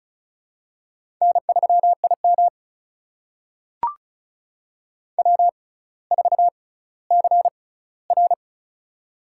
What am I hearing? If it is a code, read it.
Morse code, heard as a keyed tone, about 36 wpm, spaced as for 18 wpm, 700 Hz. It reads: N3IM W4CR
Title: USA Call Signs - 4 Characters - 6x Character Spacing - 35wpm